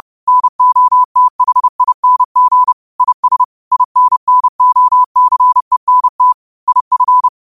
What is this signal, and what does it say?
Morse code, heard as a keyed tone, about 30 wpm, 1000 Hz. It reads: NOTHING IS INNOCENT IF
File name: qotdfiles / 30wpm